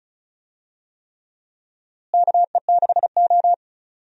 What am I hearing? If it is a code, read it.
Morse code, heard as a keyed tone, about 36 wpm, 700 Hz. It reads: KE6O
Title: USA Call Signs - 4 Characters - Repeated 3x - 35wpm